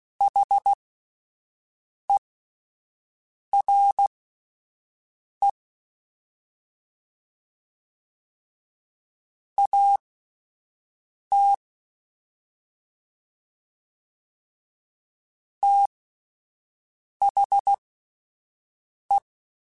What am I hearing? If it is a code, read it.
Morse code, heard as a keyed tone, about 16 wpm, spaced as for 5 wpm, 800 Hz. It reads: HERE AT THE
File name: (spaced) / QSOs Spaced-mp3s / WD0BGP 1 @10 sp